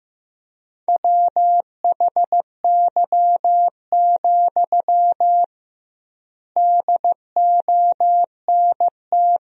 Morse code, heard as a keyed tone, about 15 wpm, 700 Hz. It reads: WHY, DONT
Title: 2x Word-Spacing - Sentences from Top 300 Words - 15wpm